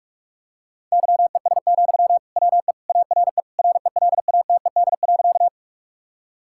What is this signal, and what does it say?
Morse code, heard as a keyed tone, about 46 wpm, 700 Hz. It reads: YES, WE ARE RELATED.
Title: Sentences from Top 1000 Words - Review Part 5 - 45wpm